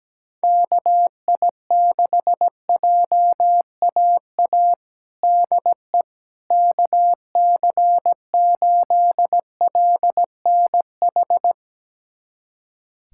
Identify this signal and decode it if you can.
Morse code, heard as a keyed tone, about 17 wpm, 700 Hz. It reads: KI6JAA DE KC8LNH
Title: USA Callsign DE Callsign 17wpm